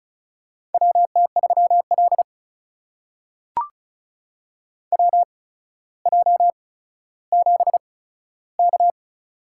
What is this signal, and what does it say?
Morse code, heard as a keyed tone, about 36 wpm, spaced as for 15 wpm, 700 Hz. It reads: WT3L WJ7K
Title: USA Call Signs - 4 Characters - 8x Character Spacing - 35wpm